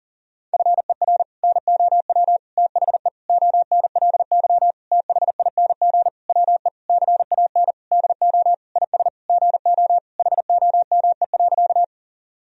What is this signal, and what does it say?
Morse code, heard as a keyed tone, about 40 wpm, 700 Hz. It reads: FER NOW THE ONLY THING WE CAN DO IS GO HOME.
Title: Sentences from Top 200 Words and Abbreviations 40wpm